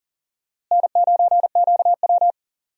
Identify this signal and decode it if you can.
Morse code, heard as a keyed tone, about 37 wpm, 700 Hz. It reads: N9QW
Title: USA Call Signs - 4 Characters - Repeated 2x - 40wpm